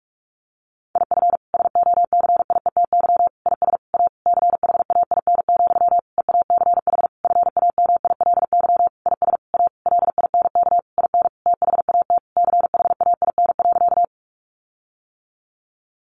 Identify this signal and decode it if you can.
Morse code, heard as a keyed tone, about 44 wpm, 700 Hz. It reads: IF SOCIETY IS A CHAIN, EACH FAMILY IS A LINK IN THAT CHAIN.
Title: Sentences from Top 1500 Words 45wpm